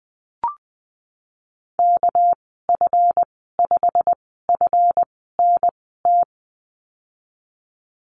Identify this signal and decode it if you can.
Morse code, heard as a keyed tone, about 20 wpm, spaced as for 15 wpm, 700 Hz. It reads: KF5FNT
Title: 207 Lesson 33g - Letter j - Call Signs 20wpm